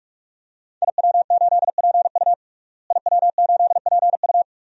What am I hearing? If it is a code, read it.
Morse code, heard as a keyed tone, about 48 wpm, 700 Hz. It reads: IW8PU IW8PU
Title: DXCC-100 Call Signs - 5 Characters - Repeated 2x - 45wpm